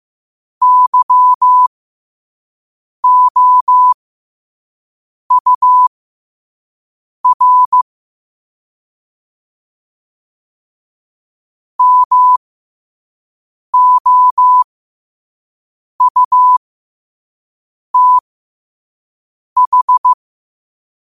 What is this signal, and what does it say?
Morse code, heard as a keyed tone, about 15 wpm, spaced as for 5 wpm, 1000 Hz. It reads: YOUR MOUTH
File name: qotdfiles / 5wpm